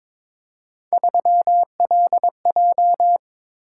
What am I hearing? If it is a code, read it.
Morse code, heard as a keyed tone, about 22 wpm, 700 Hz. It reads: 3LJ
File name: practice / 22wpm / Three Letter-Number Combinations - 22wpm